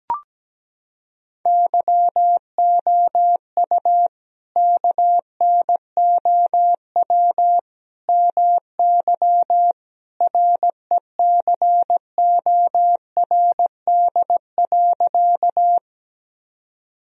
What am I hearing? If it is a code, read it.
Morse code, heard as a keyed tone, about 17 wpm, 700 Hz. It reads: YOU KNOW MY RECORD.
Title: Sentences from Top 1000 Words - Review Part 3 - 17wpm